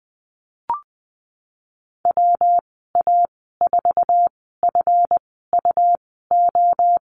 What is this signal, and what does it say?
Morse code, heard as a keyed tone, about 20 wpm, spaced as for 15 wpm, 700 Hz. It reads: WA4FUO
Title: 161 Lesson 26g - Letter g - Call Signs 20wpm